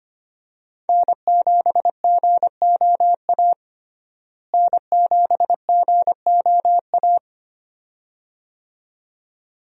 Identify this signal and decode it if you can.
Morse code, heard as a keyed tone, about 25 wpm, 700 Hz. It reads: N7GOA N7GOA
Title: USA Call Signs - 5 Characters - Repeated 2x - 25wpm